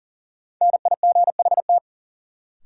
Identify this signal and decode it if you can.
Morse code, heard as a keyed tone, about 37 wpm, 700 Hz. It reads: NIGHT
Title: Sets of 4 Words - New Word Drill - 40wpm